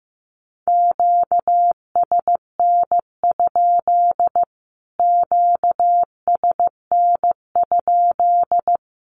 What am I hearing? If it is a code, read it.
Morse code, heard as a keyed tone, about 15 wpm, 700 Hz. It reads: QSN? QSN?
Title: Q-Codes Repeated 3x 15wpm